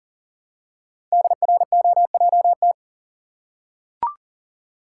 Morse code, heard as a keyed tone, about 40 wpm, 700 Hz. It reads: DROJT
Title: Code Groups - 5 Letters - 40wpm